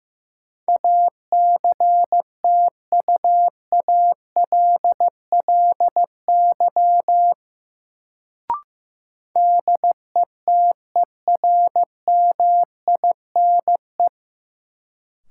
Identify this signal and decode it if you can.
Morse code, heard as a keyed tone, about 15 wpm, 700 Hz. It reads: ACTUALLY DETERMINE
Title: Sentences from Top 700 Words - New Word Drill - 15wpm